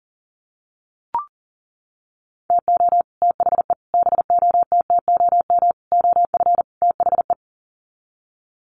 Morse code, heard as a keyed tone, about 40 wpm, 700 Hz. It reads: TO THE BOTTOM OF THE
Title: Sets of 5 Words 40wpm